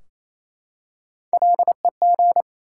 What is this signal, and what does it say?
Morse code, heard as a keyed tone, about 29 wpm, 700 Hz. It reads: LEG
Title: Three Letter Words - 28wpm